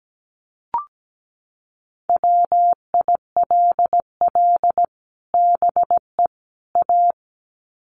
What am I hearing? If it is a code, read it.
Morse code, heard as a keyed tone, about 17 wpm, 700 Hz. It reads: WILL BE A
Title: Sets of 3 Words 17wpm